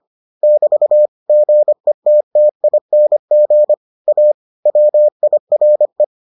Morse code, heard as a keyed tone, about 25 wpm, 600 Hz. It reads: <BT> GETTING A WIRE